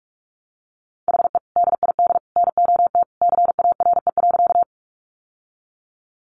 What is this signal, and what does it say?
Morse code, heard as a keyed tone, about 44 wpm, 700 Hz. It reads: HE DID NOT CARE.
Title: Sentences from Top 600 Words 45wpm